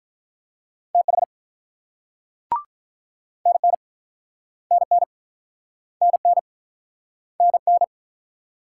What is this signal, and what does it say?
Morse code, heard as a keyed tone, about 44 wpm, 700 Hz. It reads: TH NN NN NN NN